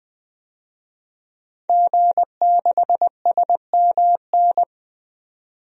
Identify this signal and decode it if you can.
Morse code, heard as a keyed tone, about 21 wpm, 700 Hz. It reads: G6SMN
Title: DXCC-100 Call Signs - 5 Characters - 20wpm